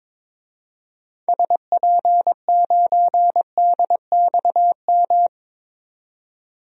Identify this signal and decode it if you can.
Morse code, heard as a keyed tone, about 23 wpm, 700 Hz. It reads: SP9DXM
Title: DXCC-100 Call Signs - 6 Characters - 22wpm